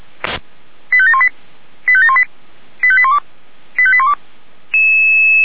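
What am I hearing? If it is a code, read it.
Five-tone selective call (ZVEI), heard as quick runs of five tones, tones of about 70 ms. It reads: 85718 85718 85812 85812
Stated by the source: ZVEI five-tone call received on an SDR.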